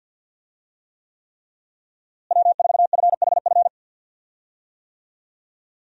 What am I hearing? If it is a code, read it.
Morse code, heard as a keyed tone, about 50 wpm, 700 Hz. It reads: W4FHF
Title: USA Call Signs - 5 Characters - Repeated 3x - 50wpm